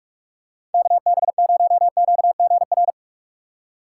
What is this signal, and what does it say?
Morse code, heard as a keyed tone, about 41 wpm, 700 Hz. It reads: KB0QGR